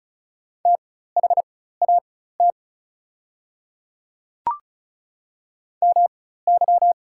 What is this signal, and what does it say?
Morse code, heard as a keyed tone, about 36 wpm, spaced as for 16 wpm, 690 Hz. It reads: THAT MY
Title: 4x Character Spacing - Top 100 Words - 35wpm